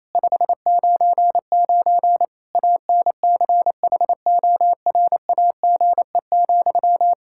Morse code, heard as a keyed tone, about 28 wpm, 700 Hz. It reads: 599 ANCHORAGE,
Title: Practice QSO - Exchange 1 - Beginning - 28wpm